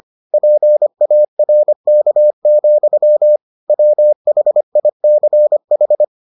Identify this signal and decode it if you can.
Morse code, heard as a keyed tone, about 25 wpm, 600 Hz. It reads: PARK, WHICH